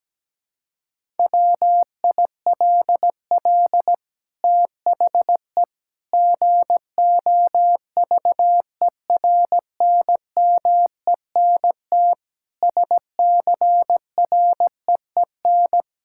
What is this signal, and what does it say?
Morse code, heard as a keyed tone, about 17 wpm, 700 Hz. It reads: WILL THE GOVERNMENT SCREEN
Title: Sentences from Top 1000 Words - Review Part 6 - 17wpm